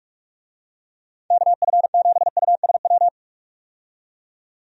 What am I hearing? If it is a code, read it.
Morse code, heard as a keyed tone, about 47 wpm, 700 Hz. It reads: KF7USW
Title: USA Call Signs - 6 Characters - 45wpm